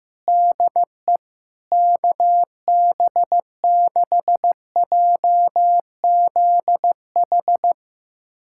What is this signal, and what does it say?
Morse code, heard as a keyed tone, about 15 wpm, 700 Hz. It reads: DE KB6JZH
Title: USA Callsign DE Callsign - 6 Characters - 15wpm